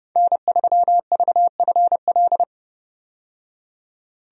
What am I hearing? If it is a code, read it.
Morse code, heard as a keyed tone, about 30 wpm, 700 Hz. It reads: N3VFL
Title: USA Call Signs - 5 Characters - 30wpm